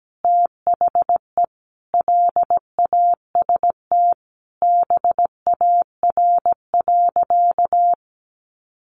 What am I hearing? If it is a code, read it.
Morse code, heard as a keyed tone, about 17 wpm, 700 Hz. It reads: THE LAST BAR.
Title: Easy Phrases 17wpm